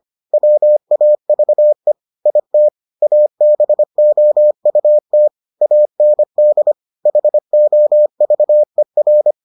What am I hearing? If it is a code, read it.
Morse code, heard as a keyed tone, about 25 wpm, 600 Hz. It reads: WAVE IT ABOUT AND HOVER